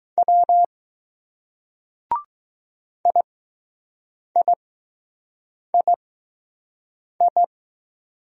Morse code, heard as a keyed tone, about 20 wpm, spaced as for 6 wpm, 700 Hz. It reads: W IIII